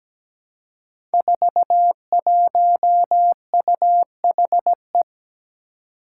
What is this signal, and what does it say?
Morse code, heard as a keyed tone, about 17 wpm, 700 Hz. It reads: 41UHE
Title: Five Challenging Letter-Number Combinations 17wpm